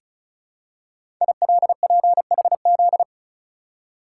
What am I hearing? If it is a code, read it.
Morse code, heard as a keyed tone, about 36 wpm, 690 Hz. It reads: ILPHZ
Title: Code Groups - 5 Letter:Numbers - 35wpm